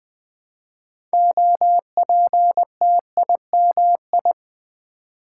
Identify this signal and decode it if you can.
Morse code, heard as a keyed tone, about 20 wpm, 700 Hz. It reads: OPTIMI
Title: Build A Word - 2 Letters At A Time - 8-Letter Words - 20wpm